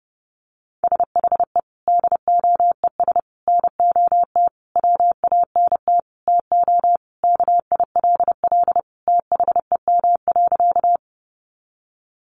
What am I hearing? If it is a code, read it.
Morse code, heard as a keyed tone, about 30 wpm, 700 Hz. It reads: SHE DOES NOT WANT TO KILL THEM.
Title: Additional Easy Phrases 30wpm